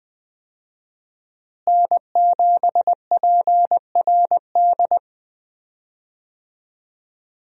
Morse code, heard as a keyed tone, about 20 wpm, 700 Hz. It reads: N7PRD
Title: USA Call Signs - 5 Characters - Repeated 2x - 20wpm